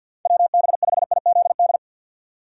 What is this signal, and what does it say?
Morse code, heard as a keyed tone, about 51 wpm, 700 Hz. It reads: WB5IZD
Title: USA Call Signs - 6 Characters - 4x Character Spacing - 50wpm